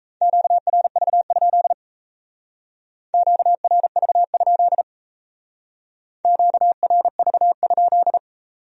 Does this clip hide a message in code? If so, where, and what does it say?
Yes: Morse code from 0.2 to 8.2 s, heard as a keyed tone, about 36 wpm, 700 Hz. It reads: QRV? QRV? QRV?